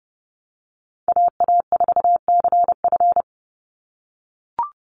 Morse code, heard as a keyed tone, about 30 wpm, 700 Hz. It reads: AA4CF
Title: Call Signs - 30wpm